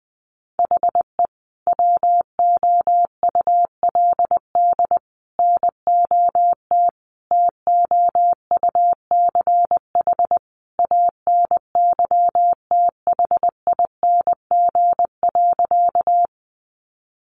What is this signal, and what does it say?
Morse code, heard as a keyed tone, about 20 wpm, 700 Hz. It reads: HE WOULD NOT TOUCH ANYTHING.
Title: Sentences from Top 1200 Words 20wpm